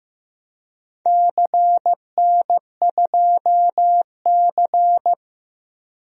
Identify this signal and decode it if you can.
Morse code, heard as a keyed tone, about 15 wpm, 700 Hz. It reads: CN2C
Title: DXCC-100 Call Signs - 4 Characters - 15wpm